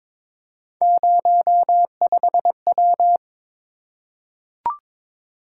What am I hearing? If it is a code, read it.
Morse code, heard as a keyed tone, about 22 wpm, 700 Hz. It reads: 05W
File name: practice / 22wpm / Three Letter-Number Combinations - 22wpm